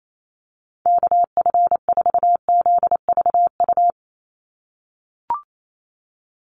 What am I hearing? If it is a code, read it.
Morse code, heard as a keyed tone, about 28 wpm, 700 Hz. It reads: KF4ZVU
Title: USA Call Signs 28wpm